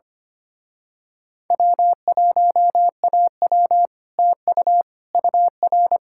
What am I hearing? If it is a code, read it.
Morse code, heard as a keyed tone, about 24 wpm, 700 Hz. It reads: W1AW TU UR
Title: Speed-Racing Common QSO Phrases - 25wpm